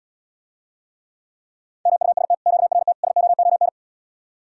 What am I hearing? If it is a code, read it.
Morse code, heard as a keyed tone, about 79 wpm, 700 Hz. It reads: MUST QRT SOON